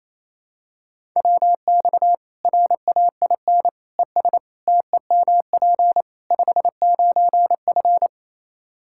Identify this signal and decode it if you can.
Morse code, heard as a keyed tone, about 27 wpm, 700 Hz. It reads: WX RAIN ES TEMP 59F
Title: Practice QSO - Exchange 2 - Info - 28wpm